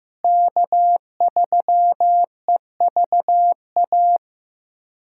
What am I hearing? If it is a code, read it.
Morse code, heard as a keyed tone, about 15 wpm, 700 Hz. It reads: K3EVA
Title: USA Callsign DE Callsign - 5 Characters - 15wpm